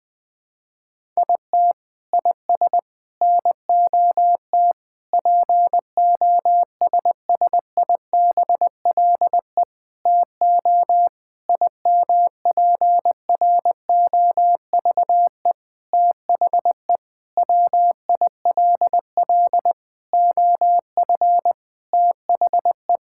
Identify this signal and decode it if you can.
Morse code, heard as a keyed tone, about 20 wpm, 700 Hz. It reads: IT IS NOT POSSIBLE TO IMPROVE THE WILL OF THE
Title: Sentences from Top 1000 Words - Review Part 5 - 20wpm